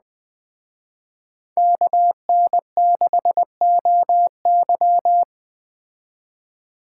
Morse code, heard as a keyed tone, about 20 wpm, 700 Hz. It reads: KN6OY